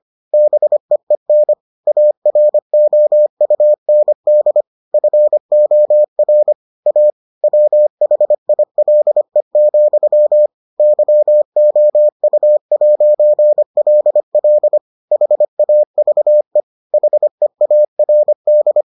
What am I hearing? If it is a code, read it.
Morse code, heard as a keyed tone, about 25 wpm, 600 Hz. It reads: BEEN AROUND FOR A WHILE, YOU'LL HAVE HEARD